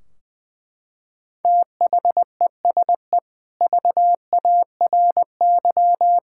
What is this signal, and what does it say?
Morse code, heard as a keyed tone, about 21 wpm, 700 Hz. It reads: THESE VARY
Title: Sentences from Top 1600 Words 20wpm